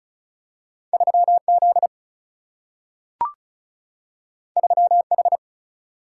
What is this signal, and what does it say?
Morse code, heard as a keyed tone, about 36 wpm, 700 Hz. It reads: 3 Z 3 H
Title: Single-Letter Single-Number 35wpm